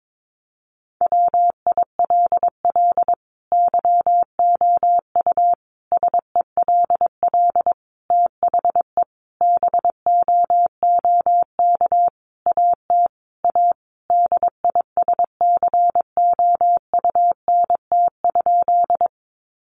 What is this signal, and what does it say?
Morse code, heard as a keyed tone, about 22 wpm, 700 Hz. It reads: WILL YOU SELL THE BOOK AT A DISCOUNT?
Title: Sentences from Top 1200 Words 22wpm